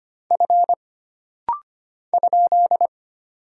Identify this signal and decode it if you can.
Morse code, heard as a keyed tone, about 25 wpm, 700 Hz. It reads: F ?
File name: Single Character (Letter:Number:Punctuation:Prosign) - Rapid-Fire - 25wpm